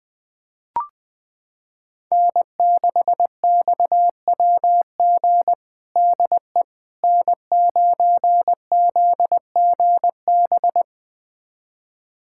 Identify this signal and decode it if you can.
Morse code, heard as a keyed tone, about 20 wpm, 700 Hz. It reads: N6XWG DE N9ZGB
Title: USA Callsign DE Callsign - 5 Characters - 20wpm